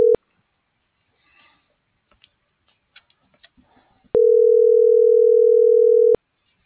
Ringback tone (North America), 440 + 480 Hz.